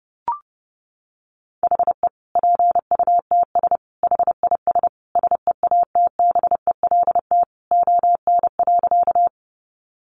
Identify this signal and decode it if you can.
Morse code, heard as a keyed tone, about 30 wpm, 700 Hz. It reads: HE PUTS HIS SEATBELT ON.
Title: Additional Easy Phrases 30wpm